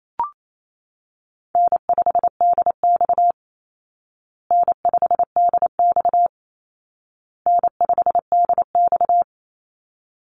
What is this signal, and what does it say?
Morse code, heard as a keyed tone, about 28 wpm, 700 Hz. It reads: N5DX N5DX N5DX
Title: Easy Call Signs Repeated 3x - 28wpm